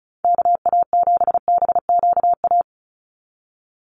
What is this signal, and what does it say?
Morse code, heard as a keyed tone, about 34 wpm, 700 Hz. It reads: KA7BQA
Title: USA Call Signs 35wpm